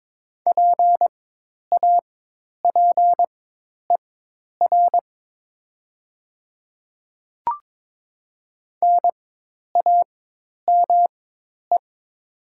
Morse code, heard as a keyed tone, about 22 wpm, spaced as for 10 wpm, 700 Hz. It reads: PAPER NAME
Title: 4x Character Spacing - Top 300 Words - 22wpm